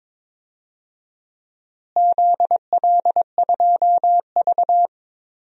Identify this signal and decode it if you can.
Morse code, heard as a keyed tone, about 21 wpm, 700 Hz. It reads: ZL2V
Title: DXCC-100 Call Signs - 4 Characters - Repeated 2x - 22wpm